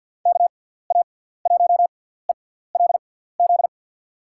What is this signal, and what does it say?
Morse code, heard as a keyed tone, about 52 wpm, spaced as for 17 wpm, 700 Hz. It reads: KA1ELZ
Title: USA Call Signs - 6 Characters - 6x Character Spacing - 50wpm